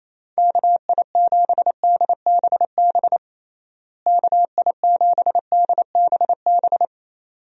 Morse code, heard as a keyed tone, about 28 wpm, 700 Hz. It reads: KI7DBB KI7DBB
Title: USA Call Signs - 6 Characters - Repeated 2x - 28wpm